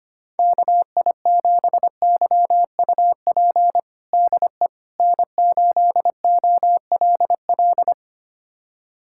Morse code, heard as a keyed tone, about 25 wpm, 700 Hz. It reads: KI7YUP DE N8OLL
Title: USA Callsign DE Callsign 25wpm